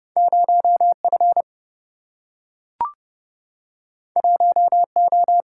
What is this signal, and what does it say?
Morse code, heard as a keyed tone, about 30 wpm, 700 Hz. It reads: 0F 1O
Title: Single-Letter Single-Number 30wpm